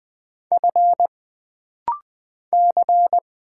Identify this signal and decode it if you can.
Morse code, heard as a keyed tone, about 20 wpm, 700 Hz. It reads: F C
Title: Single Letters - Rapid-Fire 20wpm